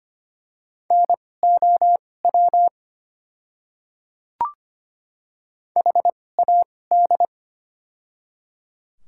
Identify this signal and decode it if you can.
Morse code, heard as a keyed tone, about 25 wpm, spaced as for 18 wpm, 690 Hz. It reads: NOW HAD
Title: 2x Character Spacing - Top 500 Words - 25wpm